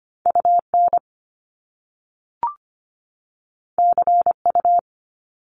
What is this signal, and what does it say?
Morse code, heard as a keyed tone, about 25 wpm, 700 Hz. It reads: UN CU
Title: Top Two Letter Combinations in Words 25wpm